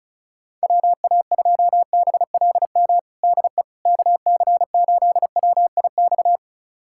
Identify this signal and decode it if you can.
Morse code, heard as a keyed tone, about 34 wpm, 700 Hz. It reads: WA2BLM DE KC8WIX
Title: USA Callsign DE Callsign 35wpm